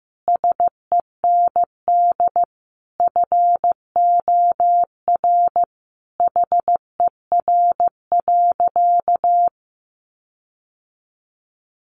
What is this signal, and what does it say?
Morse code, heard as a keyed tone, about 15 wpm, 700 Hz. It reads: SEND FOR HER.
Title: Sentences from Top 1500 Words 15wpm